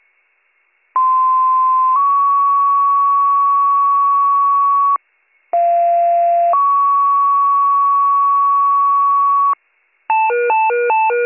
Two-tone page, heard as a one-second tone, then a three-second tone, about 1040 then 1120 Hz, about 690 then 1120 Hz. It is heard from 0.9 to 9.5 s.